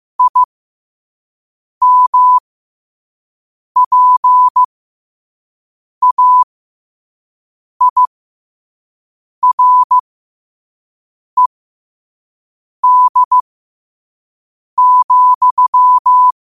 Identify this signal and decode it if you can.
Morse code, heard as a keyed tone, about 15 wpm, spaced as for 5 wpm, 1000 Hz. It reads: IMPAIRED,
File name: qotdfiles / 5wpm